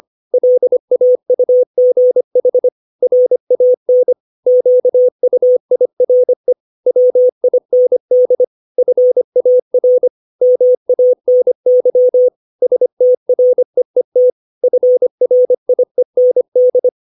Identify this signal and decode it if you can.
Morse code, heard as a keyed tone, about 25 wpm, 500 Hz. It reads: LAUGH RAN QUIRE WIND FAR MANY STREET FRIEND
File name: 632 words / 632 Words 25WPM